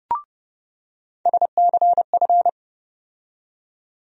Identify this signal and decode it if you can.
Morse code, heard as a keyed tone, about 30 wpm, 700 Hz. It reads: SCF
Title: USA Call Sign Postfixes - 3 Letters 30wpm